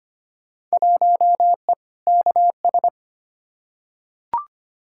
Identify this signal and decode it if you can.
Morse code, heard as a keyed tone, about 25 wpm, 700 Hz. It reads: 1E KS
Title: Field Day Exchanges - 25wpm